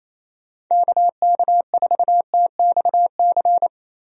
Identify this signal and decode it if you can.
Morse code, heard as a keyed tone, about 28 wpm, 700 Hz. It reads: KK4TXC
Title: USA Call Signs - 6 Characters - Repeated 3x - 28wpm